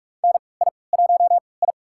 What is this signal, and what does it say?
Morse code, heard as a keyed tone, about 44 wpm, spaced as for 25 wpm, 700 Hz. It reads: NI1I
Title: USA Call Signs - 4 Characters - 3x Character Spacing - 45wpm